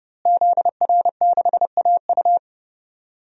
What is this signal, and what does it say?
Morse code, heard as a keyed tone, about 32 wpm, 700 Hz. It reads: ZR6AU
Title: DXCC-100 Call Signs - 5 Characters - Repeated 3x - 30wpm